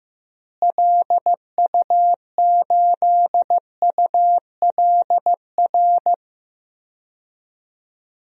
Morse code, heard as a keyed tone, about 15 wpm, 700 Hz. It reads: LU8ULR
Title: DXCC-100 Call Signs - 6 Characters - Repeated 3x - 15wpm